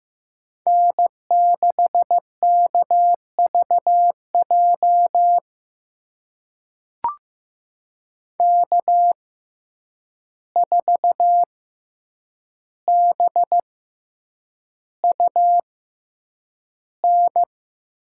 Morse code, heard as a keyed tone, about 15 wpm, spaced as for 8 wpm, 700 Hz. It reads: N6KVJ K4BUN
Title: USA Call Signs - 5 Characters - 6x Character Spacing - 15wpm